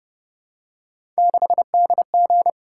Morse code, heard as a keyed tone, about 32 wpm, 700 Hz. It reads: 6DG